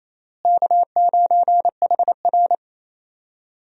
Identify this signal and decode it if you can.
Morse code, heard as a keyed tone, about 28 wpm, 700 Hz. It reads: K9HR
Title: USA Call Signs - 4 Characters - Repeated 3x - 28wpm